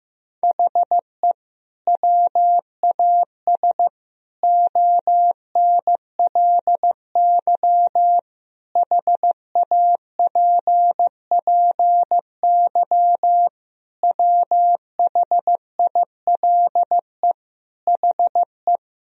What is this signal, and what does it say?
Morse code, heard as a keyed tone, about 15 wpm, 700 Hz. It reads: HE WAS ONLY HAPPY WHILE HE
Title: Sentences from Top 1000 Words - Review Part 4 - 15wpm